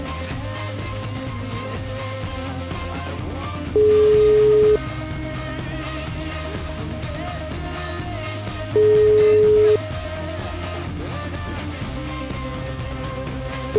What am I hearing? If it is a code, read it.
Ringback tone (Europe, 425 Hz).